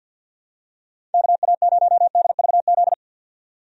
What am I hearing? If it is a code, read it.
Morse code, heard as a keyed tone, about 52 wpm, 700 Hz. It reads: KA0DVZ
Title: USA Call Signs - 6 Characters - 50wpm